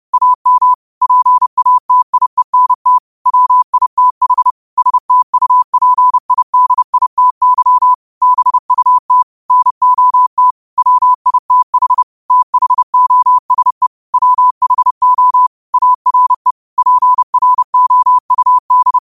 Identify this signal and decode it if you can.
Morse code, heard as a keyed tone, about 30 wpm, 1000 Hz. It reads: AM PATIENT WITH STUPIDITY BUT NOT WITH THOSE WHO ARE PROUD
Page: 30 WPM Morse code quotes